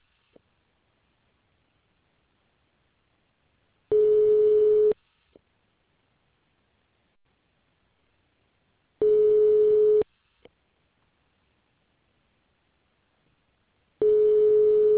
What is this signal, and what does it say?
Ringback tone (Europe, 425 Hz).